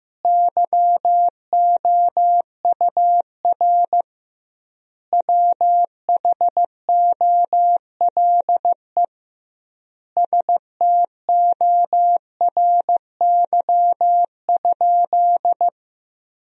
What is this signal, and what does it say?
Morse code, heard as a keyed tone, about 15 wpm, 700 Hz. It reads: YOUR WHOLE STORY?
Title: 2x Word-Spacing - Sentences from Top 400 Words - 15wpm